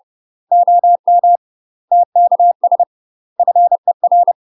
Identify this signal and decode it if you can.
Morse code, heard as a keyed tone, about 30 wpm, 700 Hz. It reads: OM TKS FER
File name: UK MASTER QSOs TRAINING / 30WPM / UK MASTER QSOs 0017